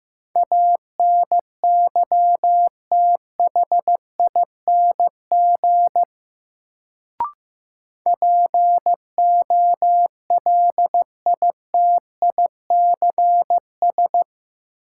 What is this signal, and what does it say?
Morse code, heard as a keyed tone, about 15 wpm, 700 Hz. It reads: ANYTHING POLITICS
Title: Sentences from Top 1000 Words - New Word Drill - 15wpm